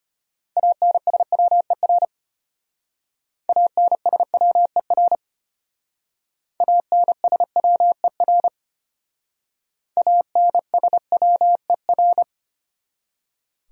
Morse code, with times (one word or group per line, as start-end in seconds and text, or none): 0.6-2.0 ANSWER
3.5-5.2 ANSWER
6.6-8.5 ANSWER
10.0-12.2 ANSWER